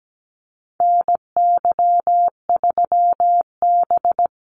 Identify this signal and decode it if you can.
Morse code, heard as a keyed tone, about 17 wpm, 700 Hz. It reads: NY3B
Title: Call Signs - 17wpm